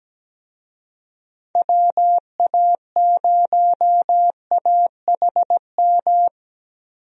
Morse code, heard as a keyed tone, about 17 wpm, 700 Hz. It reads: WA0AHM
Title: USA Call Signs - 6 Characters - Repeated 3x - 17wpm